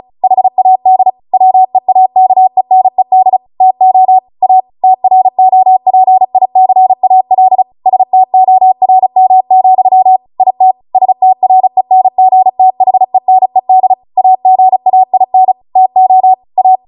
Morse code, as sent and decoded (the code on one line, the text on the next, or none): .... .- -.. / .-- . .- -.- . -. -.. / - --- / .- / - .-. --- .--. .. -.-. .- .-.. / ... - --- .-. -- --..-- / .. - / ... - .-. -. --. - .... -. -.. / .- --. .- .. -. / - --- / .-
HAD WEAKENED TO A TROPICAL STORM, IT STRENGTHENED AGAIN TO A